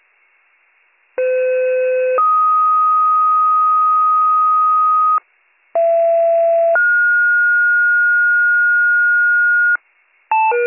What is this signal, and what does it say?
Two-tone page, heard as a one-second tone, then a three-second tone, about 510 then 1250 Hz, about 670 then 1430 Hz, with noise. It is heard from 1.2 to 9.8 s.